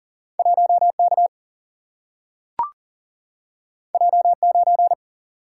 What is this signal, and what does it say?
Morse code, heard as a keyed tone, about 39 wpm, 700 Hz. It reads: 1K J9
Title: Single-Letter Single-Number 40wpm